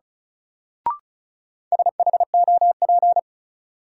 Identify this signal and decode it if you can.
Morse code, heard as a keyed tone, about 34 wpm, 700 Hz. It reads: SHOP
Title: Sentences from Top 1600 Words - New Word Drill - 35wpm